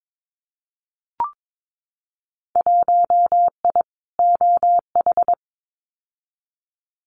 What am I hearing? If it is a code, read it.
Morse code, heard as a keyed tone, about 22 wpm, 700 Hz. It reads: 1I OH
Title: Winter Field Day Exchanges 22wpm